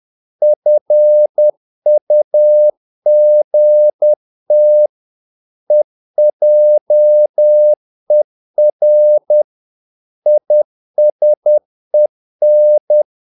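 Morse code, heard as a keyed tone, about 10 wpm, 600 Hz. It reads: FUGT EJER ISEN
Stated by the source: Morse practice of short Danish words at 10 wpm, from CW med Gnister.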